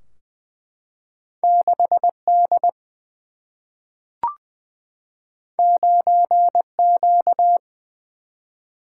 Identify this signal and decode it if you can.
Morse code, heard as a keyed tone, about 20 wpm, 700 Hz. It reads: 6D 9Q
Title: Single-Letter Single-Number 20wpm